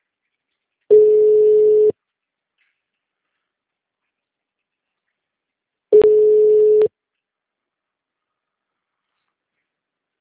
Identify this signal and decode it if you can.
Ringback tone (Europe, 425 Hz).
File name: out- -2036- - - .4897